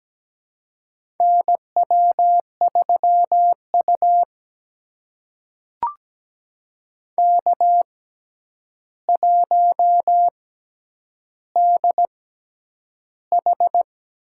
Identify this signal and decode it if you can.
Morse code, heard as a keyed tone, about 17 wpm, spaced as for 9 wpm, 700 Hz. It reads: NW3U K1DH
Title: USA Call Signs - 4 Characters - 6x Character Spacing - 17wpm